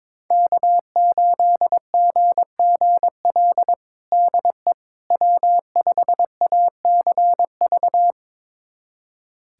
Morse code, heard as a keyed tone, about 22 wpm, 700 Hz. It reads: K8GGL DE W5ACV
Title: USA Callsign DE Callsign - 5 Characters - 22wpm